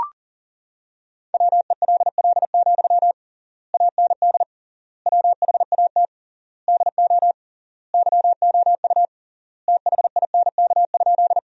Morse code, heard as a keyed tone, about 40 wpm, 700 Hz. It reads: WELL, AND WHAT DO YOU THINK?
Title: 3x Word-Spacing - Sentences from Top 200 Words - 40wpm